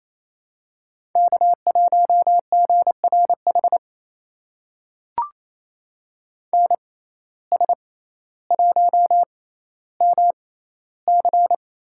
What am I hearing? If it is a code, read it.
Morse code, heard as a keyed tone, about 28 wpm, spaced as for 14 wpm, 700 Hz. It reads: K1GRH NS1MC
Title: USA Call Signs - 5 Characters - 6x Character Spacing - 28wpm